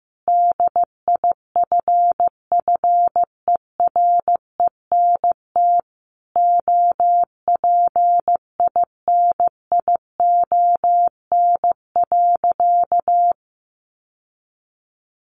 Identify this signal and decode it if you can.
Morse code, heard as a keyed tone, about 15 wpm, 700 Hz. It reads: DIFFERENT OPINION.
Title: Sentences from Top 1200 Words 15wpm